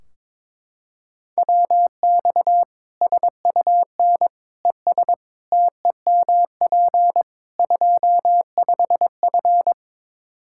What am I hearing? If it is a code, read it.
Morse code, heard as a keyed tone, about 22 wpm, 700 Hz. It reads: WX SUN ES TEMP 25F
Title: Practice QSO - Exchange 2 - Info - 22wpm